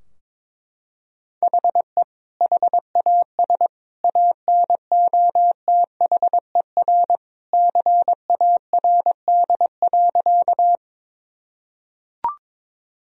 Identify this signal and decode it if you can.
Morse code, heard as a keyed tone, about 22 wpm, 700 Hz. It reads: HE HAS ANOTHER CARD.
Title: Sentences from Top 1000 Words - Review Part 4 - 22wpm